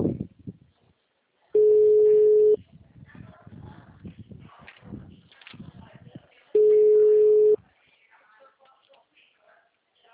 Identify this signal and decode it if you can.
Ringback tone (Europe, 425 Hz), with speech.
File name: out- -2004- - - .5